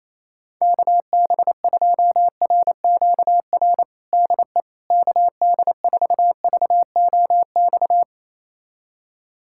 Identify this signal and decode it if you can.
Morse code, heard as a keyed tone, about 28 wpm, 700 Hz. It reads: KB2RQR DE KD4VOX
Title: USA Callsign DE Callsign - 6 Characters - 28wpm